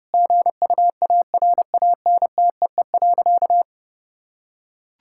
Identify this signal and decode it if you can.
Morse code, heard as a keyed tone, about 30 wpm, 700 Hz. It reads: GUARANTEE.
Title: Sentences from Top 1600 Words 30wpm